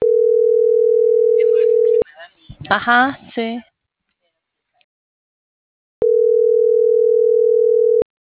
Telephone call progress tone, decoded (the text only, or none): ringback tone